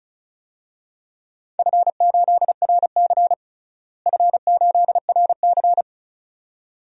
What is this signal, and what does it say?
Morse code, heard as a keyed tone, about 36 wpm, 700 Hz. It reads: F8RC F8RC
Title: DXCC-100 Call Signs - 4 Characters - Repeated 2x - 35wpm